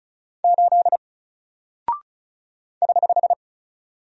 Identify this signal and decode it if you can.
Morse code, heard as a keyed tone, about 36 wpm, 700 Hz. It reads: 8 <HH>
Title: Single Character (Letter:Number:Punctuation:Prosign) Alt 35wpm